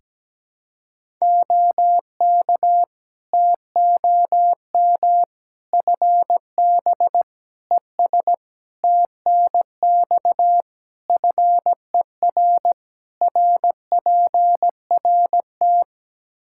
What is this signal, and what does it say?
Morse code, heard as a keyed tone, about 17 wpm, 700 Hz. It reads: OK TOM FB ES TNX FER RPRT
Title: Practice QSO - Exchange 2 - Info - 17wpm